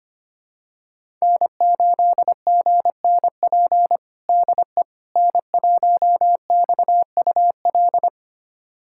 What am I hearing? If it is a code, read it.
Morse code, heard as a keyed tone, about 25 wpm, 700 Hz. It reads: N8GNP DE N1XUL
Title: USA Callsign DE Callsign - 5 Characters - 25wpm